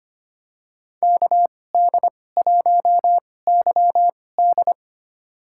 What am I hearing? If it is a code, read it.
Morse code, heard as a keyed tone, about 25 wpm, spaced as for 18 wpm, 700 Hz. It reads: KD1YD
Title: USA Call Signs - 5 Characters - 2x Character Spacing - 25wpm